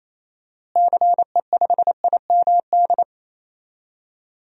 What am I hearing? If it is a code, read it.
Morse code, heard as a keyed tone, about 28 wpm, 700 Hz. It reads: CE5IMD
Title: DXCC-100 Call Signs - 6 Characters - Repeated 2x - 28wpm